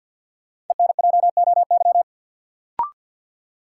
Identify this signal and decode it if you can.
Morse code, heard as a keyed tone, about 48 wpm, 700 Hz. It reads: ENJOY
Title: Sentences from Top 1300 Words - New Word Drill Alt - 50wpm